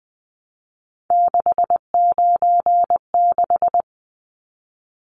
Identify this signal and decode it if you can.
Morse code, heard as a keyed tone, about 20 wpm, 700 Hz. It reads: 696
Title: Numbers - 3-Digit 20wpm